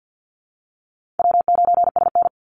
Morse code, heard as a keyed tone, about 50 wpm, 700 Hz. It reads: W9SN
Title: Easy Call Signs - 50wpm